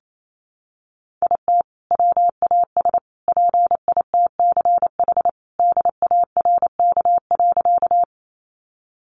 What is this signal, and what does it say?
Morse code, heard as a keyed tone, about 28 wpm, 700 Hz. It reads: IT WAS PITCH DARK.
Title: Sentences from Top 1300 Words 28wpm